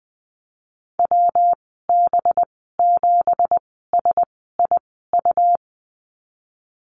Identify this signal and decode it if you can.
Morse code, heard as a keyed tone, about 19 wpm, spaced as for 14 wpm, 700 Hz. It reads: WB7SIU
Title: 186 Lesson 30g - Letter b - Call Signs 20wpm